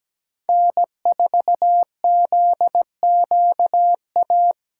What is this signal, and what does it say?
Morse code, heard as a keyed tone, about 20 wpm, 700 Hz. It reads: N4ZQA
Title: Speed Racing - USA 1x3 Call Signs 17wpm